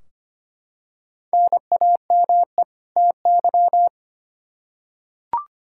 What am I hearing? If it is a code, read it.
Morse code, heard as a keyed tone, about 25 wpm, 700 Hz. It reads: NAME TY
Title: Practice QSO - Exchange 1 - Beginning - 25wpm